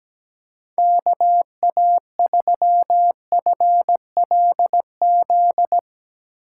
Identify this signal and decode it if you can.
Morse code, heard as a keyed tone, about 17 wpm, 700 Hz. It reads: KA3FLZ
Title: USA Call Signs - 6 Characters - 8x Character Spacing - 17wpm